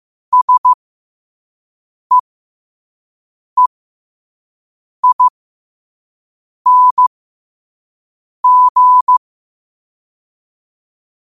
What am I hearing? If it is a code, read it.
Morse code, heard as a keyed tone, about 15 wpm, spaced as for 5 wpm, 1000 Hz. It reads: SEEING
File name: qotdfiles / 5wpm